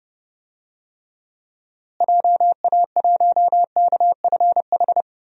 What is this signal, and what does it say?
Morse code, heard as a keyed tone, about 32 wpm, 700 Hz. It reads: JA1KFH
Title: DXCC-100 Call Signs - 6 Characters - Repeated 3x - 30wpm